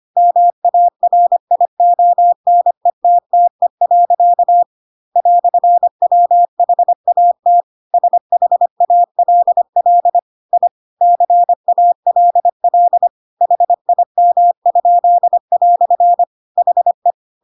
Morse code, heard as a keyed tone, about 25 wpm, 700 Hz. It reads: MARIONETTE. "WHAT SHALL I CALL HIM?" HE
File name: Pinocchio / Pinocchio ch3 25 25 ik1hge